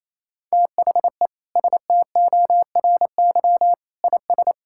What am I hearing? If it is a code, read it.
Morse code, heard as a keyed tone, about 28 wpm, 700 Hz. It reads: THE STORY IS
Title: Sentences from Top 1000 Words - Review Part 5 - 28wpm